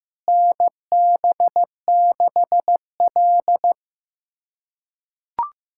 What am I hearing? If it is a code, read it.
Morse code, heard as a keyed tone, about 15 wpm, 700 Hz. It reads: NB6L
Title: USA Callsign DE Callsign - 4 Characters - 15wpm